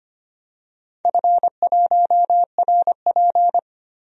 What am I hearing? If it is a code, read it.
Morse code, heard as a keyed tone, about 25 wpm, 700 Hz. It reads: F1RP
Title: DXCC-100 Call Signs - 4 Characters - 25wpm